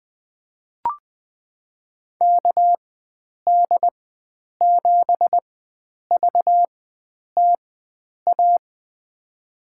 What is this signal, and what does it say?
Morse code, heard as a keyed tone, about 20 wpm, spaced as for 9 wpm, 700 Hz. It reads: KD7VTA
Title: USA Call Signs - 6 Characters - 4x Character Spacing - 20wpm